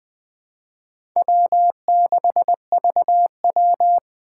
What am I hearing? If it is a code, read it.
Morse code, heard as a keyed tone, about 21 wpm, 700 Hz. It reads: W6VW
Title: USA Call Signs - 4 Characters - 3x Character Spacing - 20wpm